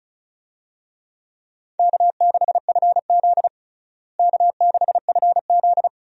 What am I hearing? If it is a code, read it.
Morse code, heard as a keyed tone, about 36 wpm, 700 Hz. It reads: K6FZ K6FZ
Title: USA Call Signs - 4 Characters - Repeated 2x - 35wpm